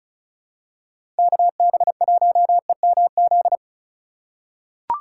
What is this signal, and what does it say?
Morse code, heard as a keyed tone, about 36 wpm, 700 Hz. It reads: KB1EMZ
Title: USA Call Signs - 6 Characters - 3x Character Spacing - 35wpm